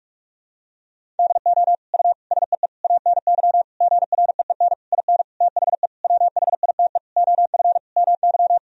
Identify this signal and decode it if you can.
Morse code, heard as a keyed tone, about 46 wpm, 700 Hz. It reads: DO U SEE ANY GREEN IN THE WHITE OF MY
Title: Sentences from Top 500 Words and Abbreviations 45wpm